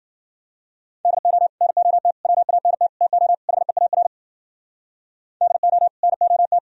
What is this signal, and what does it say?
Morse code, heard as a keyed tone, about 52 wpm, 700 Hz. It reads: DO NOT WANT TO HEAR DO NOT